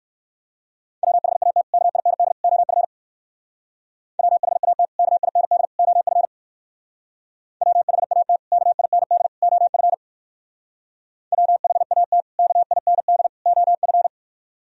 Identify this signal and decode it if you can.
Morse code, heard as a keyed tone, about 58 wpm, 700 Hz. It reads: WHAT KIND OF WHAT KIND OF WHAT KIND OF WHAT KIND OF